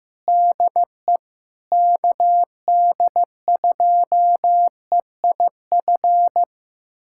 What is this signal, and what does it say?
Morse code, heard as a keyed tone, about 15 wpm, 700 Hz. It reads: DE KD2EIF
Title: USA Callsign DE Callsign - 6 Characters - 15wpm